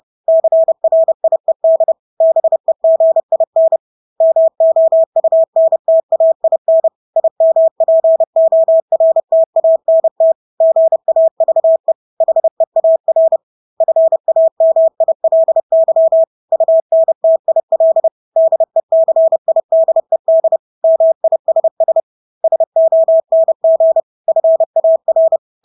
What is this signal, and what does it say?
Morse code, heard as a keyed tone, about 30 wpm, 650 Hz. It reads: CRIED BEGIN MOUNTAIN IMPORTANT GAVE HEAR FAMILY UNTIL DECIDED MISS SONG FAR
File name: Engelske ord 30-30 wpm